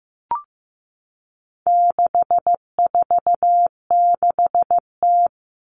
Morse code, heard as a keyed tone, about 15 wpm, 700 Hz. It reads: 646T
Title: All Cut Numbers - 4-Digit 15wpm